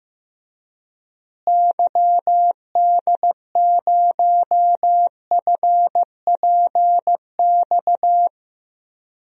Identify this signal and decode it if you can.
Morse code, heard as a keyed tone, about 15 wpm, 700 Hz. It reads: YD0FPX
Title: DXCC-100 Call Signs - 6 Characters - Repeated 3x - 15wpm